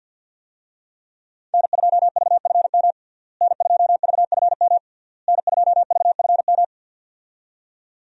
Morse code, heard as a keyed tone, about 52 wpm, 700 Hz. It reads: N2VFM N2VFM N2VFM
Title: USA Call Signs - 5 Characters - Repeated 3x - 50wpm